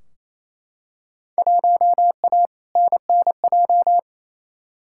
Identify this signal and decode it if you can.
Morse code, heard as a keyed tone, about 28 wpm, 700 Hz. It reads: 1A NNJ
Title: Field Day Exchanges Encore - 28wpm